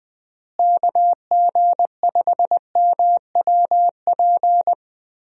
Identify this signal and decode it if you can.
Morse code, heard as a keyed tone, about 21 wpm, 700 Hz. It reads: KG5MWP
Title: USA Call Signs - 6 Characters - Repeated 3x - 20wpm